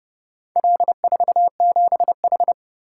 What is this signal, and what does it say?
Morse code, heard as a keyed tone, about 29 wpm, 700 Hz. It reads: L47H